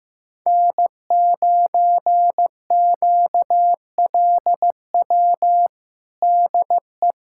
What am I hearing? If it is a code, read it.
Morse code, heard as a keyed tone, about 15 wpm, 700 Hz. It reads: N9QLW DE
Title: USA Callsign DE Callsign - 5 Characters - 15wpm